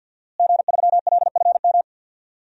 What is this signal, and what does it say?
Morse code, heard as a keyed tone, about 52 wpm, 700 Hz. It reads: G3LFM
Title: DXCC-100 Call Signs - 5 Characters - Repeated 3x - 50wpm